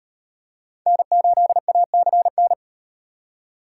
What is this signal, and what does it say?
Morse code, heard as a keyed tone, about 35 wpm, 700 Hz. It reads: N8ACN